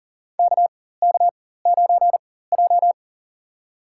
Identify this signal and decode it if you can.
Morse code, heard as a keyed tone, about 40 wpm, spaced as for 19 wpm, 700 Hz. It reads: KK9J
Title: USA Call Signs - 4 Characters - 4x Character Spacing - 40wpm